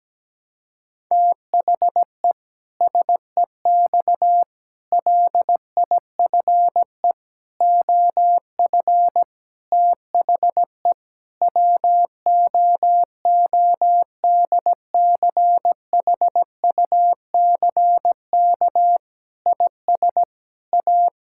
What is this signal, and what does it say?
Morse code, heard as a keyed tone, about 17 wpm, 700 Hz. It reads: THE SEX LIFE OF THE WOODCHUCK IS A
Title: Pangram Challenge v2 - 17wpm